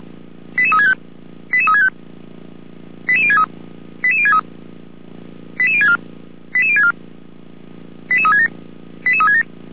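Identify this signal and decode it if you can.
Five-tone selective call (ZVEI), heard as quick runs of five tones, tones of about 70 ms. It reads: 88365 88365 88063 88063 88064 88064 88367 88367
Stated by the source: Five-tone ZVEI selective call 88191 received on an SDR.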